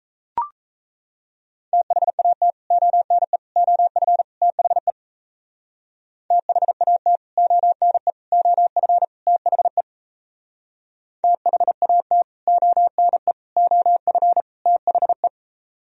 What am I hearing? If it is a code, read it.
Morse code, heard as a keyed tone, about 38 wpm, 690 Hz. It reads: THAT ONE OF THE THAT ONE OF THE THAT ONE OF THE